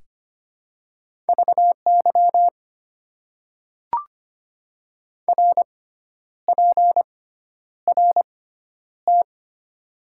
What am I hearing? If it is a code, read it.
Morse code, heard as a keyed tone, about 25 wpm, spaced as for 10 wpm, 700 Hz. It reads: VY RPRT
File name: Top 90 QSO Elements - 6x Character Spacing - 25wpm